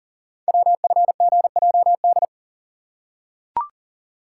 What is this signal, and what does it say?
Morse code, heard as a keyed tone, about 39 wpm, 700 Hz. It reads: WFGJD
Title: Code Groups - 5 Letter:Numbers - 40wpm